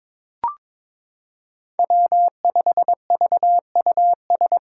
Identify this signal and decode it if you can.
Morse code, heard as a keyed tone, about 22 wpm, 700 Hz. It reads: W5VUS
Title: Five Challenging Letter-Number Combinations 22wpm